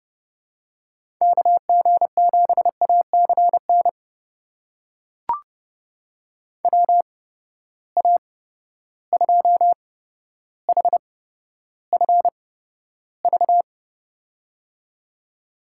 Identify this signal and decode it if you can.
Morse code, heard as a keyed tone, about 31 wpm, spaced as for 13 wpm, 700 Hz. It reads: KG7ACN WA2HFV
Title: USA Call Signs - 6 Characters - 8x Character Spacing - 30wpm